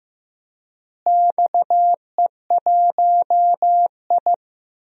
Morse code, heard as a keyed tone, about 15 wpm, 700 Hz. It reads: XE1I
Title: DXCC-100 Call Signs - 4 Characters - Repeated 3x - 15wpm